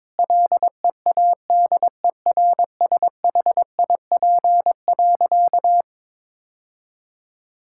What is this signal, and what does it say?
Morse code, heard as a keyed tone, about 22 wpm, 700 Hz. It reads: LEADERSHIP.
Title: Sentences from Top 1600 Words 22wpm